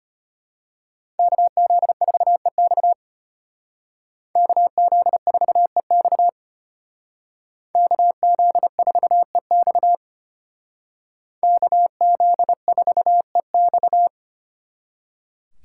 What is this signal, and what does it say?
Morse code, heard as a keyed tone, about 33 wpm, 700 Hz. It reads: KZ4EX KZ4EX KZ4EX KZ4EX